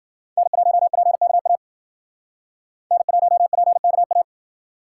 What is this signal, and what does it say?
Morse code, heard as a keyed tone, about 51 wpm, 700 Hz. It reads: N1PKA N1PKA